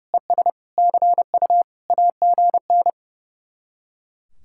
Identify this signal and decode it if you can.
Morse code, heard as a keyed tone, about 37 wpm, 700 Hz. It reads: ES CU AGN
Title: Speed-Racing Common QSO Phrases - 30wpm